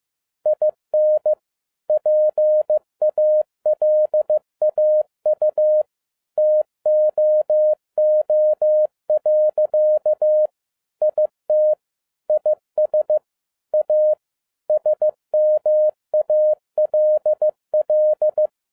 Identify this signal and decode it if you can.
Morse code, heard as a keyed tone, about 15 wpm, 610 Hz. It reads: IN PALAU TOO. IT IS A SMALL
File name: 2011-07-mp3 / n12 T8XO 15wpm